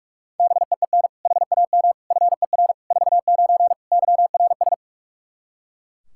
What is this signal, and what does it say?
Morse code, heard as a keyed tone, about 42 wpm, 700 Hz. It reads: BEEN HAM FER 49 YRS
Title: Practice QSO - Exchange 3 - More Info - 45wpm